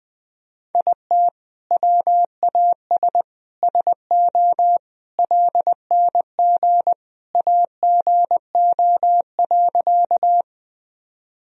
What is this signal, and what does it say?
Morse code, heard as a keyed tone, about 21 wpm, 700 Hz. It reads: IT WAS SO LNG AGO.
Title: Sentences from Top 500 Words and Abbreviations 20wpm